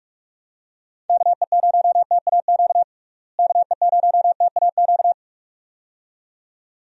Morse code, heard as a keyed tone, about 46 wpm, 700 Hz. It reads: KE0TAQ KE0TAQ
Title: USA Call Signs - 6 Characters - Repeated 2x - 45wpm